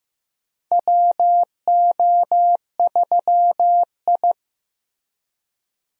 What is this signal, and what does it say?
Morse code, heard as a keyed tone, about 16 wpm, 700 Hz. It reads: WO3I
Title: USA Call Signs - 4 Characters - Repeated 3x - 15wpm